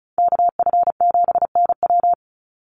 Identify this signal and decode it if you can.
Morse code, heard as a keyed tone, about 34 wpm, 700 Hz. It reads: KF7NW
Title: USA Call Signs 35wpm